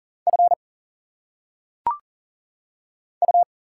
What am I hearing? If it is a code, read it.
Morse code, heard as a keyed tone, about 40 wpm, 700 Hz. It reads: F U
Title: Single Letters - Phonetic Pronunciation 40wpm